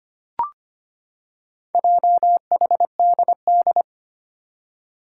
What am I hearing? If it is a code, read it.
Morse code, heard as a keyed tone, about 25 wpm, 700 Hz. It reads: JHDD
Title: Four Challenging Letter-Number Combinations 25wpm